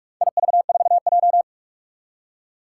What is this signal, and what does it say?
Morse code, heard as a keyed tone, about 44 wpm, 700 Hz. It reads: IV4J
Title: Four Challenging Letter-Number Combinations 45wpm